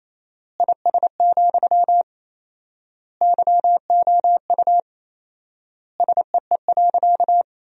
Morse code, heard as a keyed tone, about 28 wpm, 700 Hz. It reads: IS, YOU SEE.
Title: 4x Word-Spacing - Sentences from Top 100 Words - 28wpm